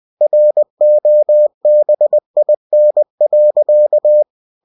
Morse code, heard as a keyed tone, about 20 wpm, 600 Hz. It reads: ROBIN.
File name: cw-books / winnie-the-pooh / winnie-20wpm-page-0043